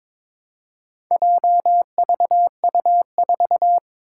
Morse code, heard as a keyed tone, about 22 wpm, 700 Hz. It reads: JVU4
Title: Four Challenging Letter-Number Combinations 22wpm